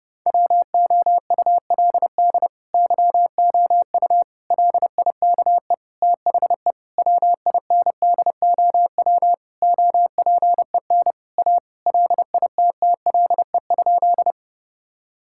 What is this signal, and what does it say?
Morse code, heard as a keyed tone, about 32 wpm, 700 Hz. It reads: WOULD YOU LIKE THE WINDOW OPEN A LITTLE?
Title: Sentences from Top 1600 Words 30wpm